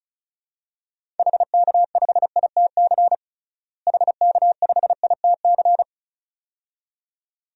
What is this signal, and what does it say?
Morse code, heard as a keyed tone, about 36 wpm, 700 Hz. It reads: HK5ITC HK5ITC
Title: DXCC-100 Call Signs - 6 Characters - Repeated 2x - 35wpm